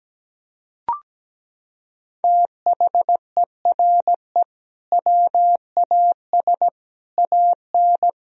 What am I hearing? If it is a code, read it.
Morse code, heard as a keyed tone, about 17 wpm, 700 Hz. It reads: THERE WAS AN
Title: Sentences from Top 1000 Words - Review Part 4 - 17wpm